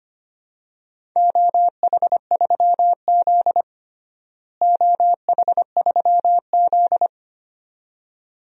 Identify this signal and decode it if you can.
Morse code, heard as a keyed tone, about 25 wpm, 700 Hz. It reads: OH3Z OH3Z
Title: DXCC-100 Call Signs - 4 Characters - Repeated 2x - 25wpm